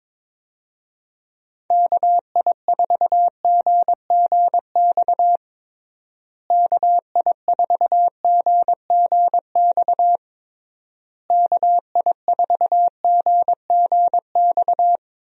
Morse code, heard as a keyed tone, about 22 wpm, 700 Hz. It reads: KI4GGX KI4GGX KI4GGX
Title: USA Call Signs - 6 Characters - Repeated 3x - 22wpm